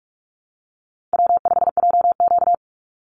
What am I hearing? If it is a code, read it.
Morse code, heard as a keyed tone, about 44 wpm, 700 Hz. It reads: W5JQ
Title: Call Signs Repeated 3x - 45wpm